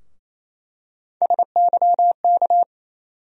Morse code, heard as a keyed tone, about 28 wpm, 700 Hz. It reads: SYK
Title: USA Call Sign Postfixes - 3 Letters 28wpm